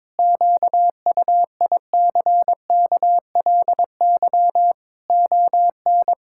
Morse code, heard as a keyed tone, about 22 wpm, 700 Hz. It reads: QUICKLY ON
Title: Pangram Challenge v2 - 22wpm